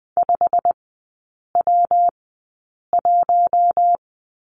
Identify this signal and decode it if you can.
Morse code, heard as a keyed tone, about 20 wpm, spaced as for 8 wpm, 700 Hz. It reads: 5W1
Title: 152 Lesson 25e - Letter p - Three Characters 20wpm